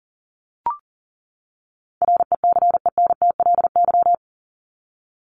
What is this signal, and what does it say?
Morse code, heard as a keyed tone, about 40 wpm, 700 Hz. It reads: RECENTLY
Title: Top 1001-1250 Words 40wpm